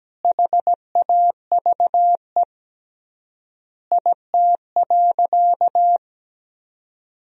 Morse code, heard as a keyed tone, about 17 wpm, 700 Hz. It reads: HAVE IT.
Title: 3x Word-Spacing - Sentences from Top 100 Words - 17wpm